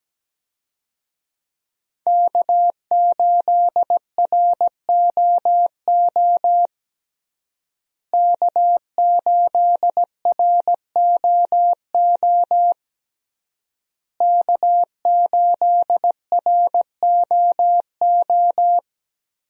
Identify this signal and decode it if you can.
Morse code, heard as a keyed tone, about 17 wpm, 700 Hz. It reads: K8ROO K8ROO K8ROO
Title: USA Call Signs - 5 Characters - Repeated 3x - 17wpm